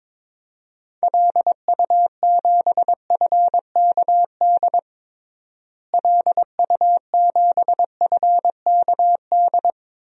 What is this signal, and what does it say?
Morse code, heard as a keyed tone, about 21 wpm, 700 Hz. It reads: LU7FKD LU7FKD
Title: DXCC-100 Call Signs - 6 Characters - Repeated 2x - 22wpm